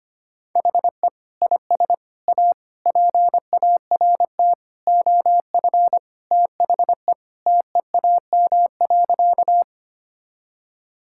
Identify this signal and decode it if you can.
Morse code, heard as a keyed tone, about 25 wpm, 700 Hz. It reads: HE IS A PART OF THE TEAM.
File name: practice / 25wpm / Sentences with 4-Word Phrases - 25wpm